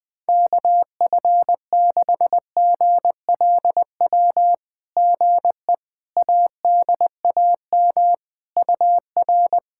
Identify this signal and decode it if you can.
Morse code, heard as a keyed tone, about 20 wpm, 700 Hz. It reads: KF6GLW GE ADAM UR
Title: SOTA QSO Practice 20wpm